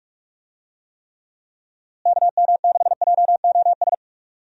Morse code, heard as a keyed tone, about 46 wpm, 700 Hz. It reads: KM6JOS
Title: USA Call Signs - 6 Characters - Repeated 3x - 45wpm